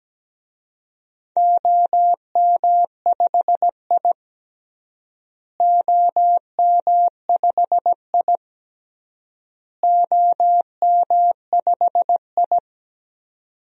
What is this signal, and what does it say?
Morse code, heard as a keyed tone, about 17 wpm, 700 Hz. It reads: OM5I OM5I OM5I
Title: DXCC-100 Call Signs - 4 Characters - Repeated 3x - 17wpm